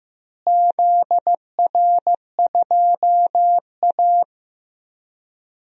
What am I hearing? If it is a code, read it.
Morse code, heard as a keyed tone, about 15 wpm, 700 Hz. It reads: ZR2A
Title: DXCC-100 Call Signs - 4 Characters - Repeated 2x - 15wpm